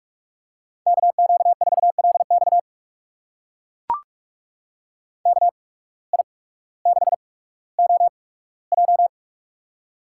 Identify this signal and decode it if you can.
Morse code, heard as a keyed tone, about 44 wpm, spaced as for 19 wpm, 700 Hz. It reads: KQ4LX KI6OJ
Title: USA Call Signs - 5 Characters - 8x Character Spacing - 45wpm